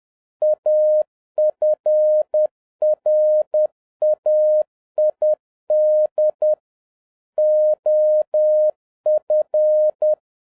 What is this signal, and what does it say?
Morse code, heard as a keyed tone, about 10 wpm, 610 Hz. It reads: AFRAID OF